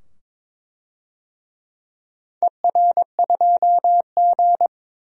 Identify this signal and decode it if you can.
Morse code, heard as a keyed tone, about 22 wpm, 700 Hz. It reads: ER2G
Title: DXCC-100 Call Signs - 4 Characters - 22wpm